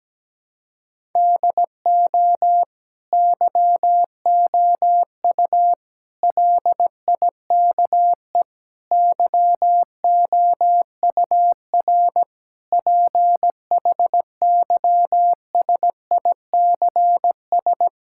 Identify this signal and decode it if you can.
Morse code, heard as a keyed tone, about 17 wpm, 700 Hz. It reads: DO YOU LIKE YOUR PHYSICS
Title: Sentences from Top 1000 Words - Review Part 5 - 17wpm